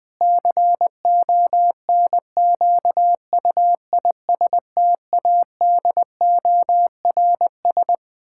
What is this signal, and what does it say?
Morse code, heard as a keyed tone, about 21 wpm, 700 Hz. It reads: CONQUISTADORS
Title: Pangram Challenge v2 - 20wpm